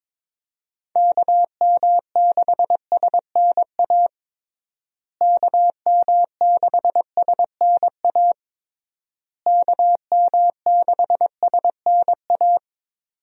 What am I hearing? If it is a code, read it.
Morse code, heard as a keyed tone, about 21 wpm, 700 Hz. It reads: KM6SNA KM6SNA KM6SNA
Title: USA Call Signs - 6 Characters - Repeated 3x - 22wpm